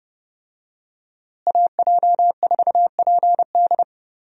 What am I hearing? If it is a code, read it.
Morse code, heard as a keyed tone, about 32 wpm, 700 Hz. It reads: AJ4PD
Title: USA Call Signs - 5 Characters - Repeated 2x - 30wpm